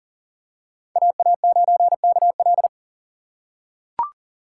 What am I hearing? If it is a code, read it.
Morse code, heard as a keyed tone, about 40 wpm, 700 Hz. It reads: AA9KL
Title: USA Call Signs - 5 Characters - 3x Character Spacing - 40wpm